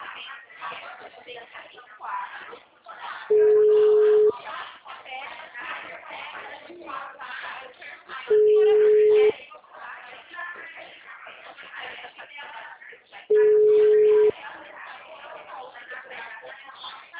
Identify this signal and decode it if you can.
Ringback tone (Europe, 425 Hz), with speech.